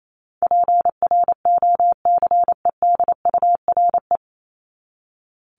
Morse code, heard as a keyed tone, about 28 wpm, 700 Hz. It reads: PROCEDURE
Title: Top 1751-2000 Words 28wpm